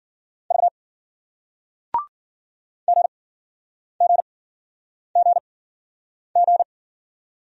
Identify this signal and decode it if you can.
Morse code, heard as a keyed tone, about 53 wpm, 700 Hz. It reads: V G G G G